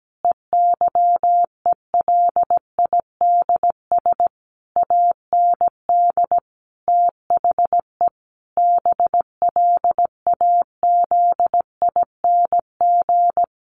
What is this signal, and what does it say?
Morse code, heard as a keyed tone, about 17 wpm, 700 Hz. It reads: EYELIDS AND THE BLAZING
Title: A Sound Of Thunder 17wpm